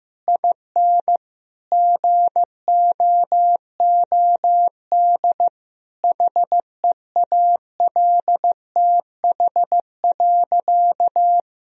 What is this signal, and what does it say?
Morse code, heard as a keyed tone, about 15 wpm, 700 Hz. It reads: IN GOOD HEALTH.
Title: Sentences from Top 1000 Words - Review Part 3 - 15wpm